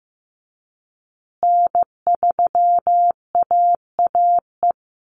Morse code, heard as a keyed tone, about 15 wpm, 700 Hz. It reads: N3AAE
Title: USA Call Signs Repeated 3x 15wpm